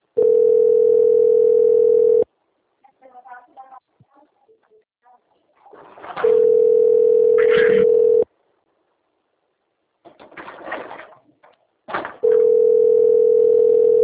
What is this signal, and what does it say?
Ringback tone (North America), 440 + 480 Hz, with speech.
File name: out- - - - - .26530